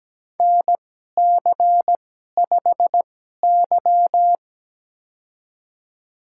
Morse code, heard as a keyed tone, about 17 wpm, spaced as for 12 wpm, 700 Hz. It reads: NC5Y
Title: USA Call Signs - 4 Characters - 2x Character Spacing - 17wpm